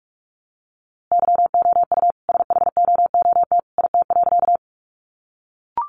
Morse code, heard as a keyed tone, about 44 wpm, 700 Hz. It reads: YOU SHOOT IT.
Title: A Sound Of Thunder 45wpm